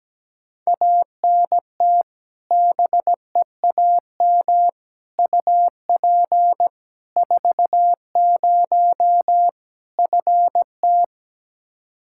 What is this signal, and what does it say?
Morse code, heard as a keyed tone, about 17 wpm, 700 Hz. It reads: ANT BEAM UP 40 FT
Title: Practice QSO - Exchange 2 - Info - 17wpm